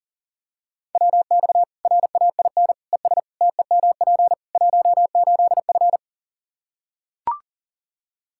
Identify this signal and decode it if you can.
Morse code, heard as a keyed tone, about 40 wpm, 700 Hz. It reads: WX RAIN ES TEMP 18F
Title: K1USN SST QSO Practice 40wpm